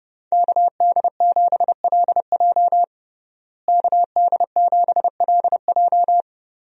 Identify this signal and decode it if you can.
Morse code, heard as a keyed tone, about 29 wpm, 700 Hz. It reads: KD7LJ KD7LJ
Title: USA Call Signs - 5 Characters - Repeated 3x - 30wpm